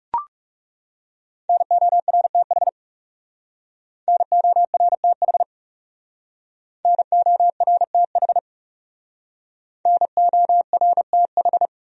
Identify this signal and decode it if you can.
Morse code, heard as a keyed tone, about 37 wpm, 700 Hz. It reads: NORTH NORTH NORTH NORTH
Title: Speed-Racing Top 300 Words 30wpm